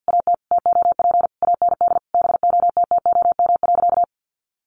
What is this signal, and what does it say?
Morse code, heard as a keyed tone, about 50 wpm, 700 Hz. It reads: AT TOP AND BOTTOM.
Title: Sentences from Top 1400 Words 50wpm